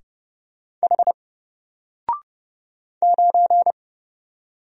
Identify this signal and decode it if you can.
Morse code, heard as a keyed tone, about 30 wpm, 700 Hz. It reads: H 9
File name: Single-Letter-Number-30wpm